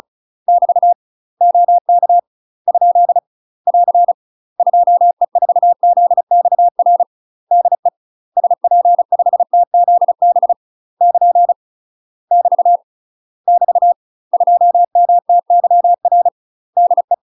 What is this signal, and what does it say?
Morse code, heard as a keyed tone, about 34 wpm, 700 Hz. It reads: <BT> OK ? <AR> 2E4ZXR DE SP5TZB <KN> <BT> <BT> 2MTYR DE
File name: UK MASTER QSOs TRAINING / 35WPM / UK MASTER QSOs 0038